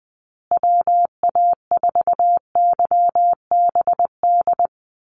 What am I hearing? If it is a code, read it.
Morse code, heard as a keyed tone, about 19 wpm, 700 Hz. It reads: WA4YBD
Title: USA Call Signs Repeated 3x 20wpm